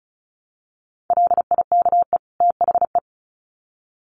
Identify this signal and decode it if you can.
Morse code, heard as a keyed tone, about 34 wpm, 700 Hz. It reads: LIKE THE
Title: Sets of 2 Words 35wpm